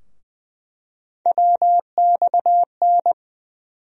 Morse code, heard as a keyed tone, about 20 wpm, 700 Hz. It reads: WXN
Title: USA Call Sign Postfixes - 3 Letters 20wpm